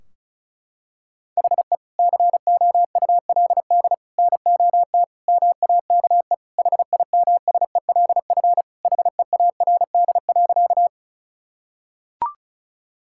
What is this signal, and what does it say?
Morse code, heard as a keyed tone, about 36 wpm, 700 Hz. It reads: HE COULD NOT MAKE HIMSELF HEARD.
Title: Sentences from Top 1600 Words 35wpm